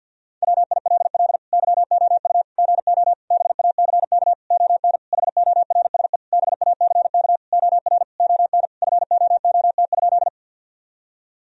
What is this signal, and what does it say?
Morse code, heard as a keyed tone, about 48 wpm, 700 Hz. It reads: WILL YOU GO BACK ON HORSE BACK OR ON FOOT?
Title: Sentences from Top 1000 Words - Review Part 3 - 50wpm